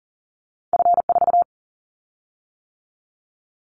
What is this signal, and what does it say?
Morse code, heard as a keyed tone, about 38 wpm, 700 Hz. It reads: F4
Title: Callsign Prefixes for Top 50 DXCC - 40wpm